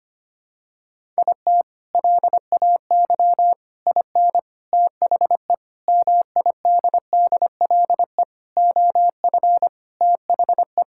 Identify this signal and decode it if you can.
Morse code, heard as a keyed tone, about 25 wpm, 700 Hz. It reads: IT LAY IN THE MIDDLE OF THE
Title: Sentences from Top 1000 Words - Review Part 6 - 25wpm